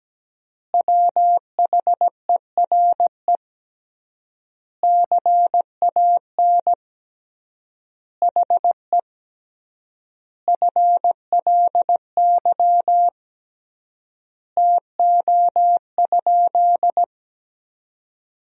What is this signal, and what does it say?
Morse code, heard as a keyed tone, about 17 wpm, 700 Hz. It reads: WHERE CAN HE FLY TO?
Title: 3x Word-Spacing - Sentences from Top 500 Words - 17wpm